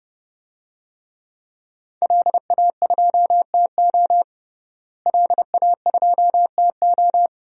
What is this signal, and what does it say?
Morse code, heard as a keyed tone, about 30 wpm, 700 Hz. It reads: LA2TO LA2TO
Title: DXCC-100 Call Signs - 5 Characters - Repeated 2x - 30wpm